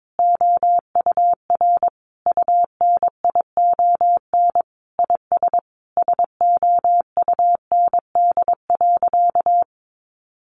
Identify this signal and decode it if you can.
Morse code, heard as a keyed tone, about 22 wpm, 700 Hz. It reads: OUR UNION IS SOUND.
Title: Sentences from Top 1400 Words 22wpm